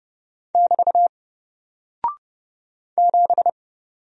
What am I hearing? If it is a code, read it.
Morse code, heard as a keyed tone, about 30 wpm, 700 Hz. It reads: <BT> 7
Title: Single Character (Letter:Number:Punctuation:Prosign) Alt 30wpm